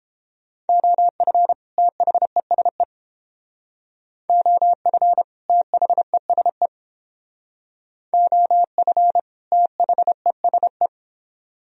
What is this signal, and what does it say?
Morse code, heard as a keyed tone, about 30 wpm, 700 Hz. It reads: OF THESE OF THESE OF THESE